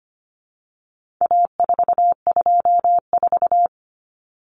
Morse code, heard as a keyed tone, about 25 wpm, 700 Hz. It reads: A424
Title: Common Cut Numbers - 4-Digit 25wpm